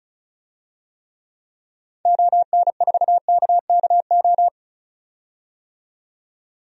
Morse code, heard as a keyed tone, about 36 wpm, 700 Hz. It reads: ON4KKO
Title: DXCC-100 Call Signs - 6 Characters - 35wpm